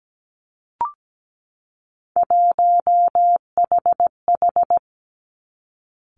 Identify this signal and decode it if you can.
Morse code, heard as a keyed tone, about 17 wpm, 700 Hz. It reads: 1HH
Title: Three Challenging Letter-Number Combinations 17wpm